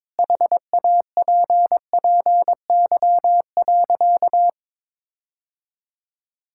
Morse code, heard as a keyed tone, about 22 wpm, 700 Hz. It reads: HAPPY.
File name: Sentences with 7-Word Phrases - 22wpm